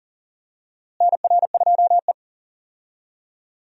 Morse code, heard as a keyed tone, about 40 wpm, 700 Hz. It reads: NR2E
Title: USA Call Signs - 4 Characters - Repeated 3x - 40wpm